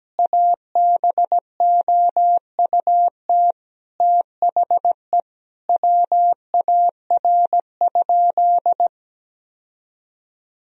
Morse code, heard as a keyed tone, about 17 wpm, 700 Hz. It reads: ABOUT THE WAR?
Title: Sentences from Top 1000 Words - Review Part 4 - 17wpm